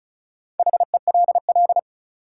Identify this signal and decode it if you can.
Morse code, heard as a keyed tone, about 36 wpm, 700 Hz. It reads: HELL
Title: Sets of 5 Words Encore - New Word Drill - 35wpm